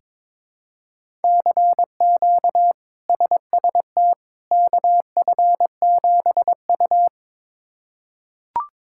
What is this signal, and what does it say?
Morse code, heard as a keyed tone, about 22 wpm, 700 Hz. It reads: CQ SST KF7U
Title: K1USN SST QSO Practice - 22wpm